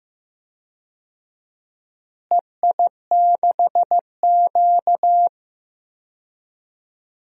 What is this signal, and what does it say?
Morse code, heard as a keyed tone, about 15 wpm, 700 Hz. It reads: EI6Q
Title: DXCC-100 Call Signs - 4 Characters - Repeated 3x - 15wpm